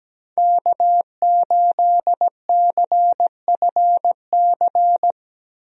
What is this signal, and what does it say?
Morse code, heard as a keyed tone, about 17 wpm, 700 Hz. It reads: K8CFC
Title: USA Call Signs - 5 Characters - 17wpm